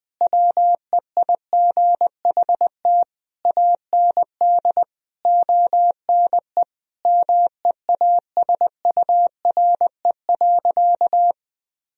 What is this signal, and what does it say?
Morse code, heard as a keyed tone, about 20 wpm, 700 Hz. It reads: WEIGHT AND ONE MEASURE.
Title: Sentences from Top 1000 Words - Review Part 3 - 20wpm